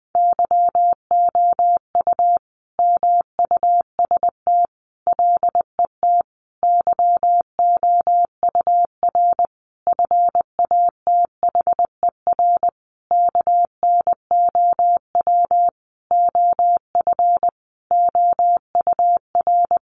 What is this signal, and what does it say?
Morse code, heard as a keyed tone, about 20 wpm, 700 Hz. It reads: YOU MUST LET YOUR FATHER KNOW OF OUR
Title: Sentences from Top 1500 Words 20wpm